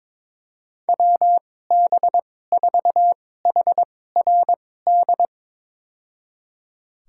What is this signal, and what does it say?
Morse code, heard as a keyed tone, about 22 wpm, spaced as for 16 wpm, 700 Hz. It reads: WB4HRD
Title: USA Call Signs - 2x Character Spacing - 22wpm